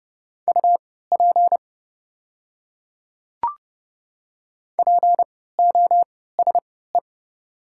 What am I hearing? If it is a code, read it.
Morse code, heard as a keyed tone, about 30 wpm, spaced as for 17 wpm, 690 Hz. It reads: UP POSE
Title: 3x Character Spacing - Top 500 Words - 30wpm